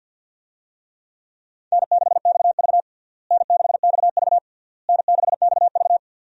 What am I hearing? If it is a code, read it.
Morse code, heard as a keyed tone, about 52 wpm, 700 Hz. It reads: N6XV N6XV N6XV
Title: USA Call Signs - 4 Characters - Repeated 3x - 50wpm